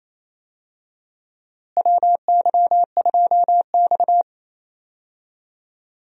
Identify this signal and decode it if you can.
Morse code, heard as a keyed tone, about 28 wpm, 700 Hz. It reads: WY2X
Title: USA Call Signs - 4 Characters - Repeated 2x - 28wpm